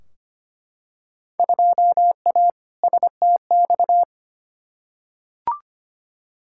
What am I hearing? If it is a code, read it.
Morse code, heard as a keyed tone, about 25 wpm, 700 Hz. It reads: 2A STX
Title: Field Day Exchanges - 25wpm